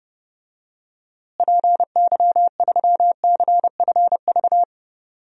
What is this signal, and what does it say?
Morse code, heard as a keyed tone, about 30 wpm, 700 Hz. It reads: PY3CFV
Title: DXCC-100 Call Signs - 6 Characters - Repeated 2x - 30wpm